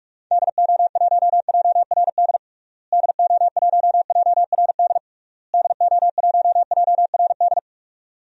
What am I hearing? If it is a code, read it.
Morse code, heard as a keyed tone, about 46 wpm, 700 Hz. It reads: DO1JRD DO1JRD DO1JRD
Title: DXCC-100 Call Signs - 6 Characters - Repeated 3x - 45wpm